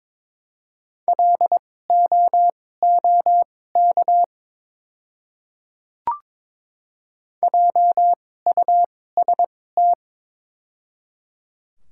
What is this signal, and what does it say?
Morse code, heard as a keyed tone, about 22 wpm, spaced as for 16 wpm, 700 Hz. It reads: LOOK JUST
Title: 2x Character Spacing - Top 300 Words - 22wpm